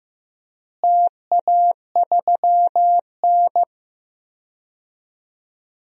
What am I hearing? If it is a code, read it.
Morse code, heard as a keyed tone, about 15 wpm, 700 Hz. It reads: TA3N
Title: DXCC-100 Call Signs - 4 Characters - Repeated 2x - 15wpm